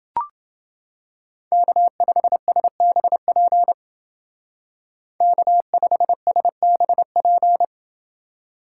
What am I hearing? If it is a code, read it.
Morse code, heard as a keyed tone, about 29 wpm, 700 Hz. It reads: K5SBP K5SBP